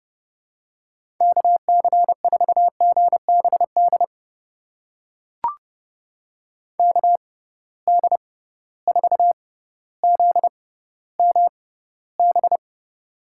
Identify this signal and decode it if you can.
Morse code, heard as a keyed tone, about 31 wpm, spaced as for 16 wpm, 700 Hz. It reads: KC4GBD KD4ZMB
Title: USA Call Signs - 6 Characters - 6x Character Spacing - 30wpm